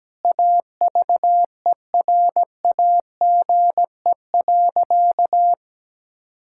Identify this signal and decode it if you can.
Morse code, heard as a keyed tone, about 17 wpm, 700 Hz. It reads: AVERAGE.
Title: Sentences from Top 1000 Words - Review Part 6 - 17wpm